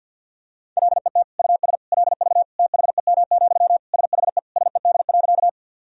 Morse code, heard as a keyed tone, about 50 wpm, 700 Hz. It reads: LET US LV THEM, SHE SED.
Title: Sentences from Top 400 Words and Abbreviations 50wpm